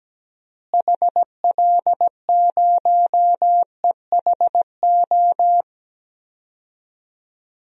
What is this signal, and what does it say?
Morse code, heard as a keyed tone, about 17 wpm, 700 Hz. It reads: HL0EHO
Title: DXCC-100 Call Signs - 6 Characters - Repeated 3x - 17wpm